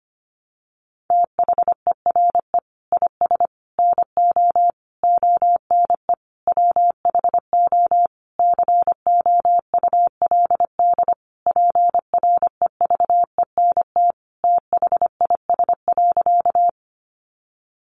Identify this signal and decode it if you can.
Morse code, heard as a keyed tone, about 25 wpm, 700 Hz. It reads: THERE IS NO ONE WHO COULD PREVENT THIS.
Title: Sentences from Top 1100 Words 25wpm